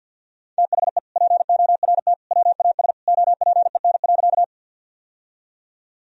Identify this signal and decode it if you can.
Morse code, heard as a keyed tone, about 51 wpm, 700 Hz. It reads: THE PORT WAS OPEN.
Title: Sentences from Top 1000 Words - Review Part 1 - 50wpm